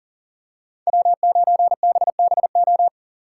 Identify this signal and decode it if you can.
Morse code, heard as a keyed tone, about 39 wpm, 700 Hz. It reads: W9BBO
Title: USA Call Signs - 5 Characters - 6x Character Spacing - 40wpm